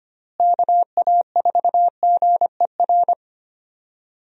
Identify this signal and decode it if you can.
Morse code, heard as a keyed tone, about 25 wpm, 700 Hz. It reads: KA4GER
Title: USA Call Signs - 6 Characters - Repeated 2x - 25wpm